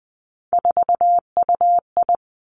Morse code, heard as a keyed tone, about 20 wpm, 700 Hz. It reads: 4UI
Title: Three Challenging Letter-Number Combinations 20wpm